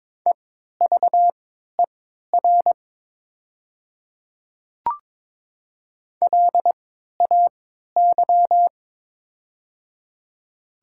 Morse code, heard as a keyed tone, about 22 wpm, spaced as for 12 wpm, 700 Hz. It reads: EVER LAY
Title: 3x Character Spacing - Top 500 Words - 22wpm